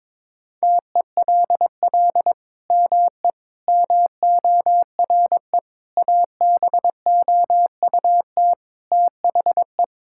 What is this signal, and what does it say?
Morse code, heard as a keyed tone, about 22 wpm, 700 Hz. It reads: TELL ME MORE ABOUT THE
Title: Sentences from Top 1000 Words - Review Part 5 - 22wpm